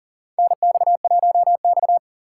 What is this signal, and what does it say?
Morse code, heard as a keyed tone, about 40 wpm, 690 Hz. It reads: NX1X
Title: USA Call Signs - 4 Characters - 2x Character Spacing - 40wpm